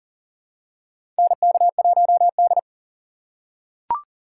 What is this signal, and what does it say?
Morse code, heard as a keyed tone, about 40 wpm, 700 Hz. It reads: NK1D
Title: USA Call Signs - 4 Characters - 2x Character Spacing - 40wpm